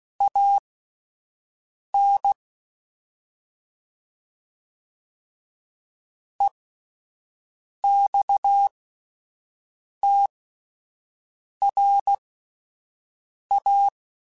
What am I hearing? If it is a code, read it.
Morse code, heard as a keyed tone, about 16 wpm, spaced as for 5 wpm, 800 Hz. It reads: AN EXTRA
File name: (spaced) / QSOs Spaced-mp3s / WA7QOX 2 @10 sp